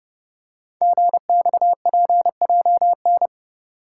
Morse code, heard as a keyed tone, about 30 wpm, 700 Hz. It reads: GXPJN
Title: Code Groups - 5 Letter:Numbers - 30wpm